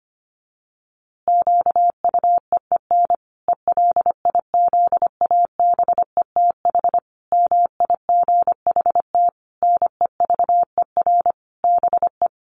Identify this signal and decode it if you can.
Morse code, heard as a keyed tone, about 25 wpm, 700 Hz. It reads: QUEEN ELIZABETH MIGHT NEVER BE
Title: A Sound Of Thunder 25wpm